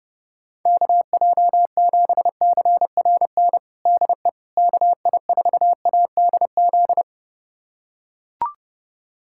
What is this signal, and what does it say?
Morse code, heard as a keyed tone, about 32 wpm, 700 Hz. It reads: KJ7CRN DE KI4ADZ
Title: USA Callsign DE Callsign - 6 Characters - 30wpm